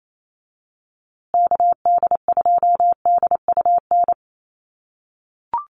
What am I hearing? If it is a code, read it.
Morse code, heard as a keyed tone, about 28 wpm, 700 Hz. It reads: KD2DUN
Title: USA Call Signs Repeated 2x 28wpm